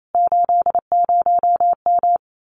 Morse code, heard as a keyed tone, about 28 wpm, 700 Hz. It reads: 80M
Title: Common QSO Phrases Encore 28wpm